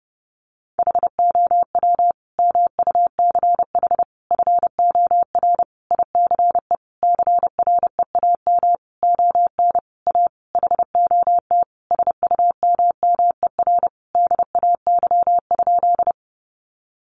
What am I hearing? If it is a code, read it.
Morse code, heard as a keyed tone, about 28 wpm, 700 Hz. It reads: HOW MUCH FOR ICE CREAM ON A HOT SUMMER DAY?
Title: Sentences from Top 1500 Words 30wpm